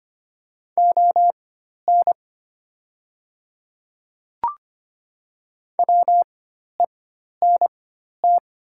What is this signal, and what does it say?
Morse code, heard as a keyed tone, about 25 wpm, spaced as for 12 wpm, 700 Hz. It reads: ON WENT
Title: 4x Character Spacing - Top 200 Words - 25wpm